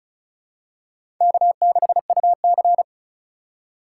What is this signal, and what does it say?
Morse code, heard as a keyed tone, about 36 wpm, 700 Hz. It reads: K6UC
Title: USA Call Signs - 4 Characters - 4x Character Spacing - 35wpm